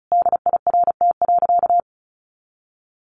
Morse code, heard as a keyed tone, about 34 wpm, 700 Hz. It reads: DIRT.
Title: A Sound Of Thunder 35wpm